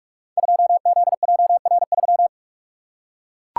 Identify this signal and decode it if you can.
Morse code, heard as a keyed tone, about 44 wpm, 700 Hz. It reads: 2ZJR3
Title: Code Groups - 5 Letter:Numbers - 45wpm